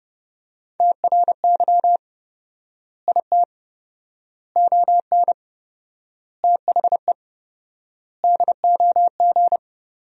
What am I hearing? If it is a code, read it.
Morse code, heard as a keyed tone, about 30 wpm, 700 Hz. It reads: TRY IT ON THE DOG
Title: 4x Word-Spacing - Sentences from Top 400 Words - 30wpm